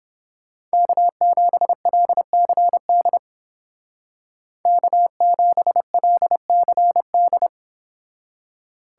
Morse code, heard as a keyed tone, about 31 wpm, 700 Hz. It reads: K7LCD K7LCD